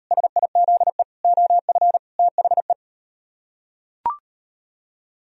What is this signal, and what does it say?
Morse code, heard as a keyed tone, about 39 wpm, 700 Hz. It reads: SIZE OF THE